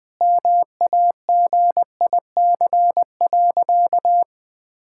Morse code, heard as a keyed tone, about 20 wpm, 700 Hz. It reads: MAGIC.
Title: Sentences with 4-Word Phrases - 20wpm